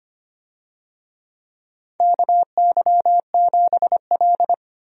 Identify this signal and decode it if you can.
Morse code, heard as a keyed tone, about 25 wpm, 700 Hz. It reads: KY7L
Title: USA Call Signs - 4 Characters - Repeated 2x - 25wpm